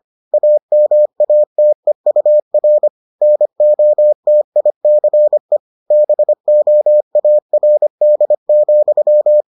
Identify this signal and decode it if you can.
Morse code, heard as a keyed tone, about 25 wpm, 600 Hz. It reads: AMATEUR NOTICE BOARD,